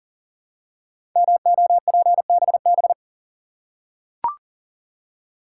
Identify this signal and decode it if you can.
Morse code, heard as a keyed tone, about 39 wpm, 700 Hz. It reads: MOPBB
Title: Code Groups - 5 Letters - 40wpm